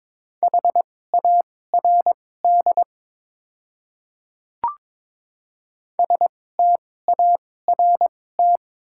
Morse code, heard as a keyed tone, about 22 wpm, spaced as for 16 wpm, 700 Hz. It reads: HARD START
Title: 2x Character Spacing - Top 500 Words - 22wpm